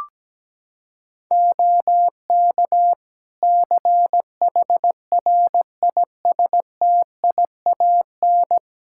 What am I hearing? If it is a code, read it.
Morse code, heard as a keyed tone, about 17 wpm, 700 Hz. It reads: OK CHRISTIAN
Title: Practice QSO - Exchange 4 - Ending - 17wpm